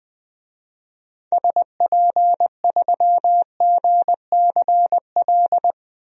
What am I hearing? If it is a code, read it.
Morse code, heard as a keyed tone, about 20 wpm, 700 Hz. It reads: SP3GCL
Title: DXCC-100 Call Signs - 6 Characters - Repeated 2x - 20wpm